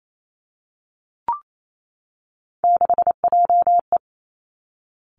Morse code, heard as a keyed tone, about 28 wpm, 700 Hz. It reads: 6JE
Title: Three Challenging Letter-Number Combinations 28wpm